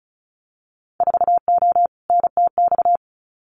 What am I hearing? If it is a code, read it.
Morse code, heard as a keyed tone, about 36 wpm, 700 Hz. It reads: 4O NTX
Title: Winter Field Day Exchanges 35wpm